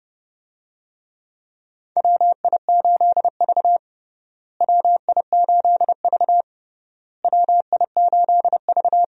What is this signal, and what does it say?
Morse code, heard as a keyed tone, about 30 wpm, 700 Hz. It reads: WI8V WI8V WI8V
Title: USA Call Signs - 4 Characters - Repeated 3x - 30wpm